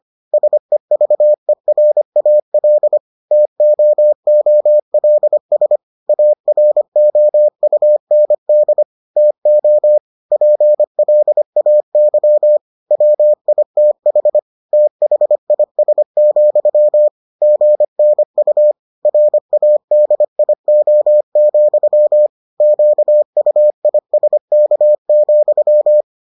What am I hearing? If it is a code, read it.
Morse code, heard as a keyed tone, about 25 wpm, 600 Hz. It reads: SEVERAL TOOLS AROUND TO PLAY WITH THIS, GNU RADIO, QUISK,